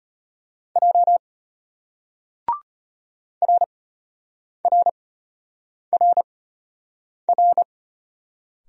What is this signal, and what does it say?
Morse code, heard as a keyed tone, about 34 wpm, spaced as for 8 wpm, 700 Hz. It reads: J RRRR